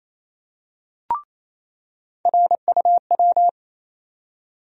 Morse code, heard as a keyed tone, about 28 wpm, 700 Hz. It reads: RUW
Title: USA Call Sign Postfixes - 3 Letters 28wpm